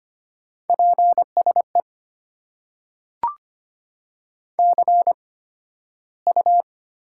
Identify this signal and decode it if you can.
Morse code, heard as a keyed tone, about 25 wpm, spaced as for 13 wpm, 700 Hz. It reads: PSE CU
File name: Top 90 QSO Elements - 8x Character Spacing - 25wpm